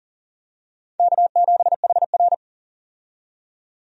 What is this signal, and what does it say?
Morse code, heard as a keyed tone, about 40 wpm, 700 Hz. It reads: K7HR
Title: USA Call Signs - 4 Characters - 40wpm